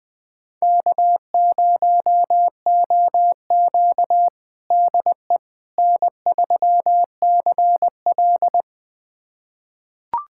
Morse code, heard as a keyed tone, about 20 wpm, 700 Hz. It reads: K0OQ DE N3CL
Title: USA Callsign DE Callsign - 4 Characters - 20wpm